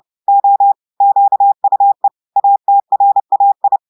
Morse code, heard as a keyed tone, about 30 wpm, 800 Hz. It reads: O QUE ATRAI
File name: audios / portugais / portugais texte en clair 2 vitesse 30 mots minutes